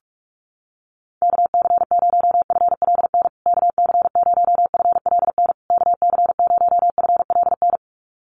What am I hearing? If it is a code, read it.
Morse code, heard as a keyed tone, about 44 wpm, 700 Hz. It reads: KC0FLN KC0FLN KC0FLN
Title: USA Call Signs Repeated 3x 45wpm